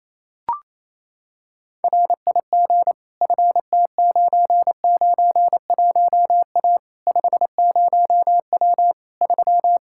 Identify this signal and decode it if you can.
Morse code, heard as a keyed tone, about 28 wpm, 700 Hz. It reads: RIG FT991A 50W 3
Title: Practice QSO - Exchange 2 - Info - 28wpm